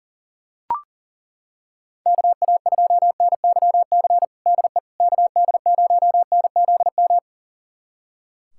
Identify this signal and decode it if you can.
Morse code, heard as a keyed tone, about 40 wpm, 700 Hz. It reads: KA2NYC DE KD0NZM
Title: USA Callsign DE Callsign - 6 Characters - 40wpm